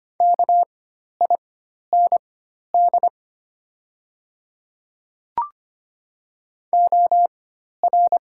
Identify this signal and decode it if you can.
Morse code, heard as a keyed tone, about 25 wpm, spaced as for 12 wpm, 700 Hz. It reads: KIND OR
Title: 4x Character Spacing - Top 200 Words - 25wpm